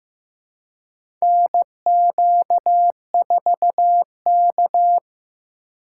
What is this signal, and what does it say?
Morse code, heard as a keyed tone, about 16 wpm, 700 Hz. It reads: NQ4K
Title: USA Call Signs - 4 Characters - 4x Character Spacing - 15wpm